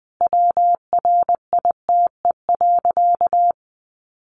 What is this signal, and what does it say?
Morse code, heard as a keyed tone, about 19 wpm, 700 Hz. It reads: WRITE.
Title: Sentences from Top 800 Words 20wpm